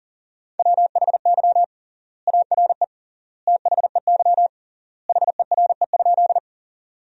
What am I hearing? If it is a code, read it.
Morse code, heard as a keyed tone, about 37 wpm, 700 Hz. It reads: WHY ARE THEY HERE?
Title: 3x Word-Spacing - Sentences from Top 200 Words - 40wpm